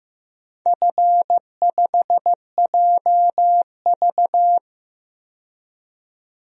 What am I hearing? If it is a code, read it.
Morse code, heard as a keyed tone, about 15 wpm, 700 Hz. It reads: F5JV
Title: DXCC-100 Call Signs - 4 Characters - 15wpm